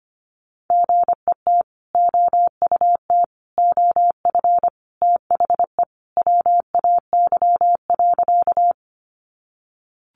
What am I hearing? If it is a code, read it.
Morse code, heard as a keyed tone, about 25 wpm, 700 Hz. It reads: GET OUT OF THE WAY.
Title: Sentences from Top 200 Words 25wpm